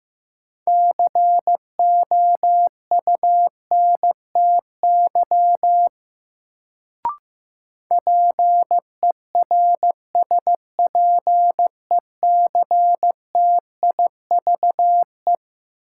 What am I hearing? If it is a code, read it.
Morse code, heard as a keyed tone, about 15 wpm, 700 Hz. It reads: COUNTY PERSPECTIVE
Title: Sentences from Top 1200 Words - New Word Drill - 15wpm